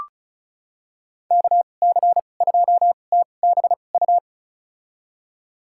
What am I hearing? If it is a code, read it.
Morse code, heard as a keyed tone, about 36 wpm, spaced as for 25 wpm, 700 Hz. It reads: KC2TBU
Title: USA Call Signs - 2x Character Spacing - 35wpm